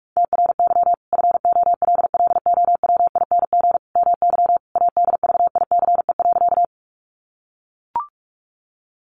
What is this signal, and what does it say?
Morse code, heard as a keyed tone, about 44 wpm, 700 Hz. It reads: TRY FOLLOWING MY ADVICE.
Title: Sentences from Top 1100 Words 45wpm